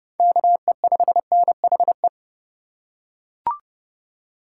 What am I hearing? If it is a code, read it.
Morse code, heard as a keyed tone, about 29 wpm, 700 Hz. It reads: KE5NHE